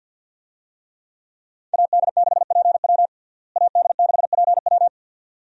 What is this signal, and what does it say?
Morse code, heard as a keyed tone, about 47 wpm, 700 Hz. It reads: AD6PW AD6PW
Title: USA Call Signs - 5 Characters - Repeated 2x - 50wpm